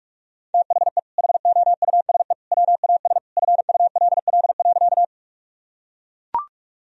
Morse code, heard as a keyed tone, about 43 wpm, 700 Hz. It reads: THE HOUSE WAS FULL.
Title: Sentences from Top 1000 Words - Review Part 3 - 45wpm